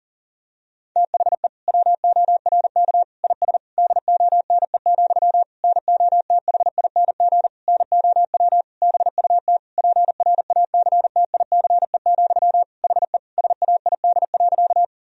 Morse code, heard as a keyed tone, about 40 wpm, 700 Hz. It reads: THE WORK IS DONE, NOTHING NOW BUT PRACTICE, HE SAID.
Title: Sentences from Top 1000 Words - Review Part 4 - 40wpm